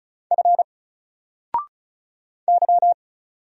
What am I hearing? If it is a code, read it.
Morse code, heard as a keyed tone, about 36 wpm, 700 Hz. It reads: F Y
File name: Single-Letter-Number-35wpm